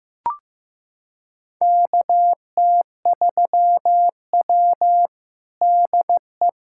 Morse code, heard as a keyed tone, about 15 wpm, 700 Hz. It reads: KT3W DE
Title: USA Callsign DE Callsign - 4 Characters - 15wpm